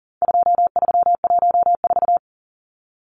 Morse code, heard as a keyed tone, about 40 wpm, 700 Hz. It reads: 2314
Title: Numbers - 4-Digit 40wpm